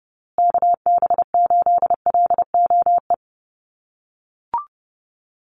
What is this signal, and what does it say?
Morse code, heard as a keyed tone, about 30 wpm, 700 Hz. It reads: KB8LOE
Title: USA Call Signs Repeated 2x 30wpm